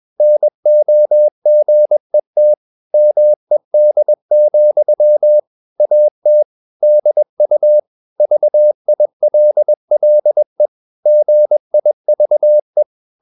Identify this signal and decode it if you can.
Morse code, heard as a keyed tone, about 21 wpm, 600 Hz. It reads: NOGET MED, AT DU VILLE GIVE
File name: Ferie QRS 21wpm